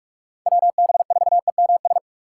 Morse code, heard as a keyed tone, about 44 wpm, 700 Hz. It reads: WB4EMS
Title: USA Call Signs - 6x Character Spacing - 45wpm